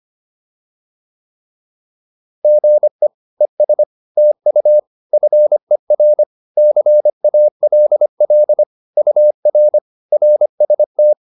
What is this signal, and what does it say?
Morse code, heard as a keyed tone, about 25 wpm, 600 Hz. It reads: GE ES TU FER CALL UR RST